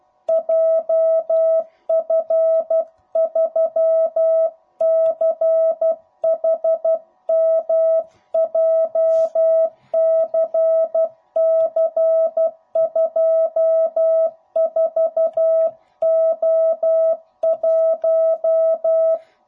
Morse code, heard as a keyed tone, about 12 wpm, 660 Hz. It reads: JF3CHM JCC24O1
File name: jh8cbh / 2021cw / cw05b